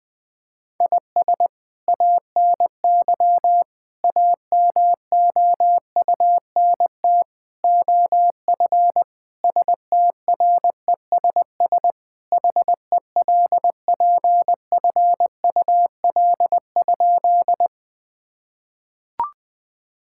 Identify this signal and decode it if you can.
Morse code, heard as a keyed tone, about 21 wpm, 700 Hz. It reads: IS ANY AMOUNT OF STRESS HELPFUL?
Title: Sentences from Top 1000 Words - Review Part 6 - 20wpm